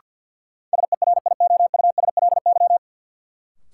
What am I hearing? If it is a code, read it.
Morse code, heard as a keyed tone, about 54 wpm, 700 Hz. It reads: SERIOUSLY
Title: Sentences from Top 1200 Words - New Word Drill Alt - 50wpm